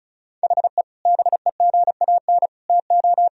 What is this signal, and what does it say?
Morse code, heard as a keyed tone, about 34 wpm, 700 Hz. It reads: HE BEGAN TO
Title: Sentences from Top 1000 Words - Review Part 1 - 35wpm